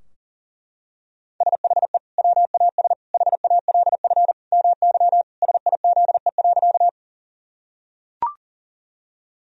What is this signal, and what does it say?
Morse code, heard as a keyed tone, about 40 wpm, 700 Hz. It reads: SHE WAS HALF MY SIZE.
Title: Sentences from Top 1000 Words - Review Part 2 - 40wpm